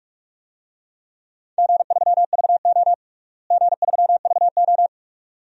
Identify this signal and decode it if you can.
Morse code, heard as a keyed tone, about 46 wpm, 700 Hz. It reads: G3VO G3VO
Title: DXCC-100 Call Signs - 4 Characters - Repeated 2x - 45wpm